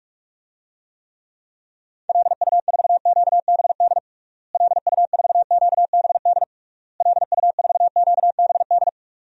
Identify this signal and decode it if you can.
Morse code, heard as a keyed tone, about 46 wpm, 700 Hz. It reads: LU4QBD LU4QBD LU4QBD
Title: DXCC-100 Call Signs - 6 Characters - Repeated 3x - 45wpm